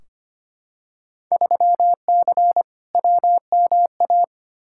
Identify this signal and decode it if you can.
Morse code, heard as a keyed tone, about 25 wpm, 700 Hz. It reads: 3C WMA
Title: Field Day Exchanges Encore - 25wpm